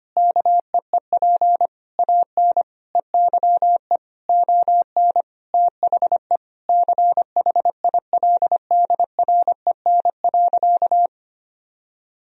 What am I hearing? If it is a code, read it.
Morse code, heard as a keyed tone, about 25 wpm, 700 Hz. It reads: KEEP AN EYE ON THE CHILDREN.
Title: Sentences from Top 300 Words and Abbreviations 25wpm